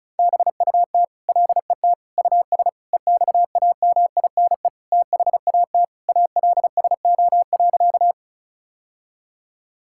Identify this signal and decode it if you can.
Morse code, heard as a keyed tone, about 36 wpm, 700 Hz. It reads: BUT LET US EXAMINE THAT ALSO.
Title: Sentences from Top 1600 Words 35wpm